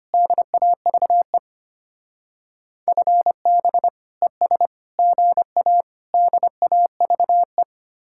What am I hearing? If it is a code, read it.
Morse code, heard as a keyed tone, about 33 wpm, 700 Hz. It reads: DAVE FB ES GA DAVE